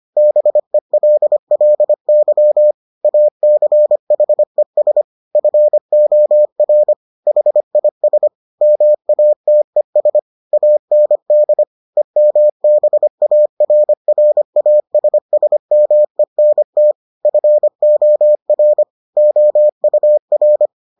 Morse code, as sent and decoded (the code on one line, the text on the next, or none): -... .-.. .-.. -.-- / .- -.-. .... ... / ..-. --- .-. / .... .. ... / -- .- - . ... / .- -. -.. / . -- -... .- .-. .-. .- ... ... -- . -. - / ..-. --- .-. / --- ..- .-.
BELLY ACHES FOR HIS MATES AND EMBARRASSMENT FOR OUR